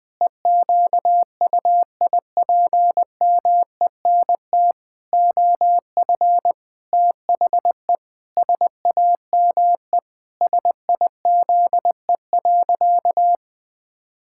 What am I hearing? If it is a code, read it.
Morse code, heard as a keyed tone, about 20 wpm, 700 Hz. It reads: EQUIPMENT OF THE SAME SIZE.
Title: Pangram Challenge v2 - 20wpm